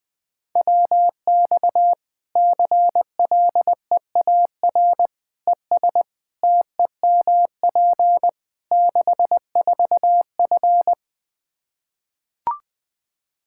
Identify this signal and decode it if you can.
Morse code, heard as a keyed tone, about 20 wpm, 700 Hz. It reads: WX CLEAR ES TEMP 64F
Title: K1USN SST QSO Practice 20wpm